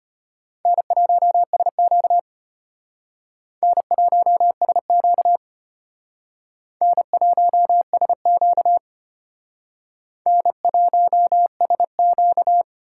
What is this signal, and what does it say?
Morse code, heard as a keyed tone, about 32 wpm, 700 Hz. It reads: N1SQ N1SQ N1SQ N1SQ